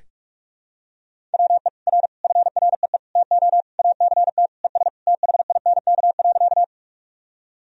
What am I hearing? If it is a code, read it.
Morse code, heard as a keyed tone, about 41 wpm, 700 Hz. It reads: WE R FREE TO ACT ES THINK.
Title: Sentences from Top 500 Words and Abbreviations 45wpm